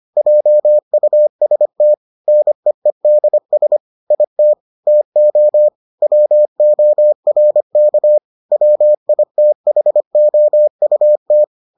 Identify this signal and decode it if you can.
Morse code, heard as a keyed tone, about 25 wpm, 600 Hz. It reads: JUST NEEDS IT TO WORK WITHOUT